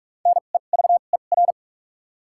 Morse code, heard as a keyed tone, about 43 wpm, 700 Hz. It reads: N E V E R
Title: 2x Character Spacing - Top 300 Words - 45wpm